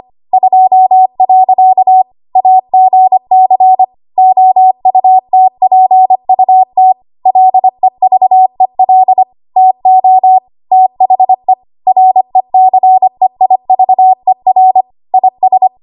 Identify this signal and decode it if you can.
Morse code, heard as a keyed tone, about 25 wpm, 750 Hz. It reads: 2. AGC OUTPUT LEVEL TO THE RECEIVER IS
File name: Archive / 25 WPM / 25WPM